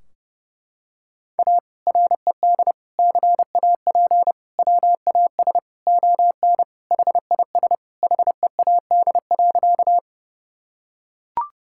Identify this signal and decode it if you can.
Morse code, heard as a keyed tone, about 30 wpm, 700 Hz. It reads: A RED CAP WAS ON HIS HEAD.
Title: Sentences from Top 1600 Words 30wpm